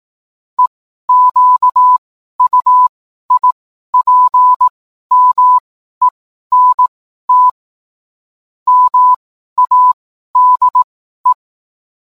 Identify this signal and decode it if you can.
Morse code, heard as a keyed tone, about 18 wpm, spaced as for 12 wpm, 1000 Hz. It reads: EQUIPMENT MADE
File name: cwblog / cw mp3 directory / mem morse 12